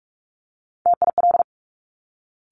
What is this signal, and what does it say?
Morse code, heard as a keyed tone, about 46 wpm, 700 Hz. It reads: TIL
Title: Top Three Letter Combinations in Words 45wpm